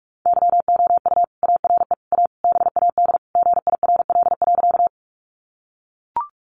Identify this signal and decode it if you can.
Morse code, heard as a keyed tone, about 44 wpm, 700 Hz. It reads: YOU ARE A BAD GIRL.
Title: Sentences from Top 700 Words 45wpm